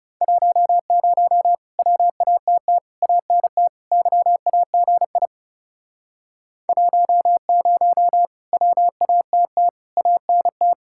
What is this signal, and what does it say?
Morse code, heard as a keyed tone, about 36 wpm, 700 Hz. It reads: 10 WATT ANT YAGI 10 WATT ANT
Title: Speed-Racing Common QSO Phrases - 30wpm